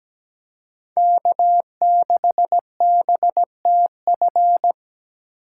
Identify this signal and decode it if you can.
Morse code, heard as a keyed tone, about 20 wpm, 700 Hz. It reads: K6BTF
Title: Speed Racing - USA 1x3 Call Signs 17wpm